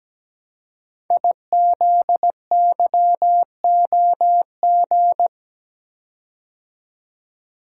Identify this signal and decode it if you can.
Morse code, heard as a keyed tone, about 17 wpm, 700 Hz. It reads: IZYOG
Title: Code Groups - 5 Letters - 17wpm